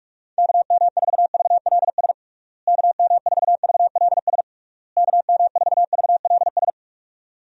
Morse code, heard as a keyed tone, about 48 wpm, 700 Hz. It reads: KM4VLS KM4VLS KM4VLS
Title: USA Call Signs - 6 Characters - Repeated 3x - 45wpm